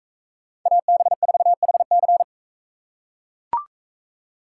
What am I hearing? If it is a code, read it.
Morse code, heard as a keyed tone, about 40 wpm, 700 Hz. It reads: AB4HC